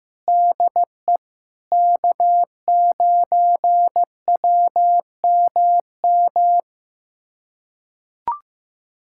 Morse code, heard as a keyed tone, about 16 wpm, 700 Hz. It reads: DE K9WMM
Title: USA Callsign DE Callsign - 5 Characters - 15wpm